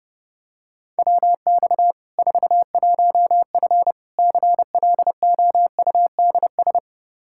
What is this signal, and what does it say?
Morse code, heard as a keyed tone, about 32 wpm, 700 Hz. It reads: WX 41F CLOUDS
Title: Practice QSO - Exchange 2 - Info - 30wpm